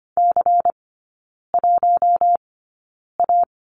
Morse code, heard as a keyed tone, about 25 wpm, spaced as for 9 wpm, 700 Hz. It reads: C1A
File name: course / 25:10wpm / 093 Lesson 16e - Letter c - Three Characters 25wpm